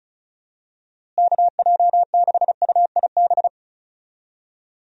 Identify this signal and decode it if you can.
Morse code, heard as a keyed tone, about 34 wpm, 700 Hz. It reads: KJ6UIB
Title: USA Call Signs - 6 Characters - Repeated 3x - 35wpm